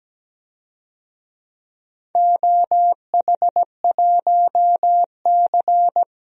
Morse code, heard as a keyed tone, about 17 wpm, 700 Hz. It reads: OH1C
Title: DXCC-100 Call Signs - 4 Characters - Repeated 3x - 17wpm